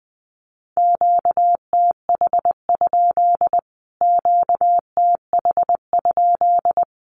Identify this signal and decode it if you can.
Morse code, heard as a keyed tone, about 19 wpm, 700 Hz. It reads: QTH? QTH?
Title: Q-Codes Repeated 3x 20wpm